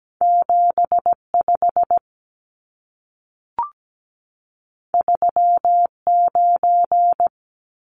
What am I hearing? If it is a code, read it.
Morse code, heard as a keyed tone, about 17 wpm, 700 Hz. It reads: 75 39
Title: Numbers - 2-Digit 17wpm